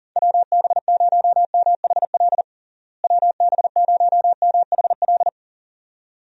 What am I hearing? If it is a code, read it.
Morse code, heard as a keyed tone, about 40 wpm, 700 Hz. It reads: WB0MHL WB0MHL
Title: USA Call Signs - 6 Characters - Repeated 2x - 40wpm